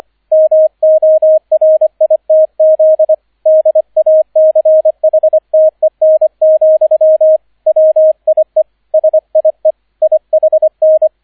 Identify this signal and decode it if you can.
Morse code, heard as a keyed tone, about 25 wpm, 630 Hz, with radio noise. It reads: MORITZ DACHTEN, WIE SIE IHN